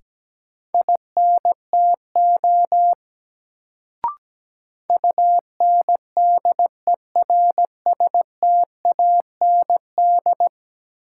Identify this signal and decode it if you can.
Morse code, heard as a keyed tone, about 17 wpm, 700 Hz. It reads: INTO UNDERSTAND
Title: Sentences from Top 600 Words - New Word Drill - 17wpm